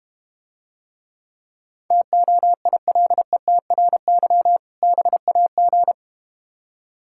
Morse code, heard as a keyed tone, about 32 wpm, 700 Hz. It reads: TOILETRY BAG